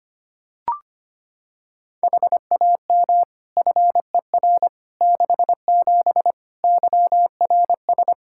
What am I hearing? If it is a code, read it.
Morse code, heard as a keyed tone, about 25 wpm, 700 Hz. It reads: HAM FER 67 YRS
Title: Practice QSO - Exchange 3 - More Info - 25wpm